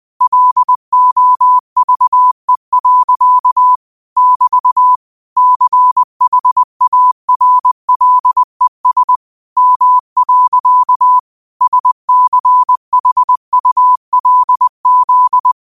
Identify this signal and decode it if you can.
Morse code, heard as a keyed tone, about 20 wpm, 1000 Hz. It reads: LOVE. <BT> CHARLES M. SCHULZ
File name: qotdfiles / 20wpm